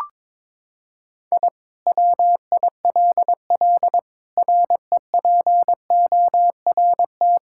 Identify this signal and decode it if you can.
Morse code, heard as a keyed tone, about 22 wpm, 700 Hz. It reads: I WILL REPORT
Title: Sentences from Top 1000 Words - Review Part 6 - 22wpm